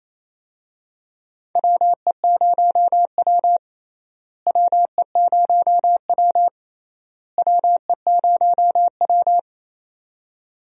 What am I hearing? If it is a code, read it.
Morse code, heard as a keyed tone, about 28 wpm, 700 Hz. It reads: WE0W WE0W WE0W
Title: USA Call Signs - 4 Characters - Repeated 3x - 28wpm